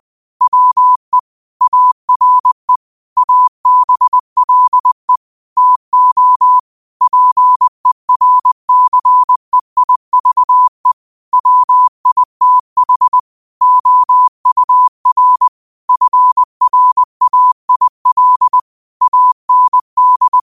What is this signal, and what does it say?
Morse code, heard as a keyed tone, about 20 wpm, 1000 Hz. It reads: WE ARE ABLE TO PERCEIVE WITH OUR FRAIL AND